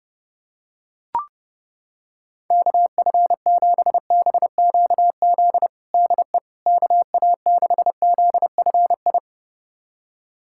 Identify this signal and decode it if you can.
Morse code, heard as a keyed tone, about 32 wpm, 700 Hz. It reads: KF7BQZ DE KA6ZFI
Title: USA Callsign DE Callsign - 6 Characters - 30wpm